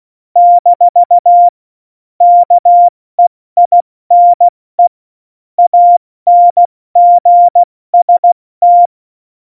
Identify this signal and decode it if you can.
Morse code, heard as a keyed tone, about 16 wpm, spaced as for 14 wpm, 700 Hz. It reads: - KEINE ANGST
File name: Morsekurs / Witze / Witz 19. Dezember 2017, 70 F80